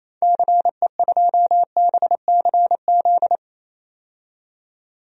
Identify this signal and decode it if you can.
Morse code, heard as a keyed tone, about 27 wpm, 700 Hz. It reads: CE2BCZ
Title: DXCC-100 Call Signs - 6 Characters - 28wpm